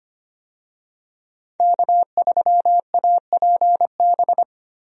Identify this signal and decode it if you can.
Morse code, heard as a keyed tone, about 25 wpm, 700 Hz. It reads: K3APB
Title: USA Call Signs - 5 Characters - Repeated 2x - 25wpm